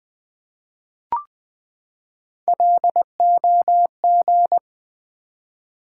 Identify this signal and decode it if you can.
Morse code, heard as a keyed tone, about 20 wpm, 700 Hz. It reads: LOG
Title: Three Letter Words - 20wpm